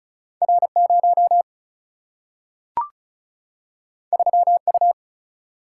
Morse code, heard as a keyed tone, about 35 wpm, 700 Hz. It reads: R0 3U
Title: Single-Letter Single-Number 35wpm